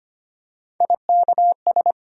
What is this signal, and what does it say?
Morse code, heard as a keyed tone, about 25 wpm, 700 Hz. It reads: IKS
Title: USA Call Sign Postfixes - 3 Letters 25wpm